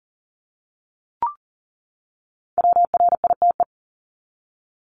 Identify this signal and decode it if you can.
Morse code, heard as a keyed tone, about 40 wpm, 700 Hz. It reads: WRITE
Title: ING Suffix - Top 100 Words with Base Words 40wpm